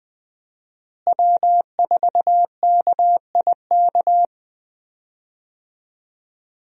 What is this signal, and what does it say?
Morse code, heard as a keyed tone, about 20 wpm, 700 Hz. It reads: W4KIK
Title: USA Call Signs - 5 Characters - Repeated 2x - 20wpm